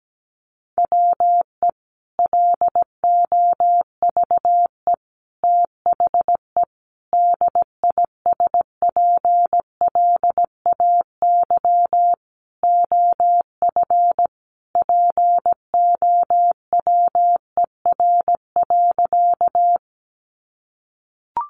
Morse code, heard as a keyed tone, about 17 wpm, 700 Hz. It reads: WE LOVE THE DISPLAY OF POWER.
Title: Sentences from Top 1200 Words 17wpm